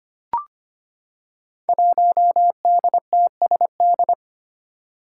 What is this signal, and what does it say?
Morse code, heard as a keyed tone, about 25 wpm, 700 Hz. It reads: 1DTSD
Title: Five Challenging Letter-Number Combinations 25wpm